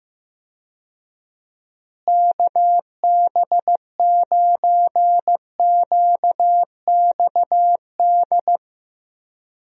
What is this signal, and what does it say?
Morse code, heard as a keyed tone, about 15 wpm, 700 Hz. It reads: KB9QXD
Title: USA Call Signs - 6 Characters - Repeated 3x - 15wpm